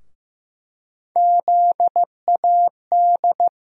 Morse code, heard as a keyed tone, about 15 wpm, 700 Hz. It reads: ZAD
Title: USA Call Sign Postfixes - 3 Letters 15wpm